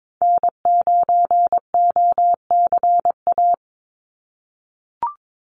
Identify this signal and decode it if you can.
Morse code, heard as a keyed tone, about 22 wpm, 700 Hz. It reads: N9OCA
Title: USA Call Signs 22wpm